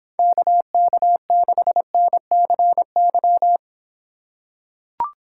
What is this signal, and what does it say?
Morse code, heard as a keyed tone, about 26 wpm, 700 Hz. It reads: KK6NCY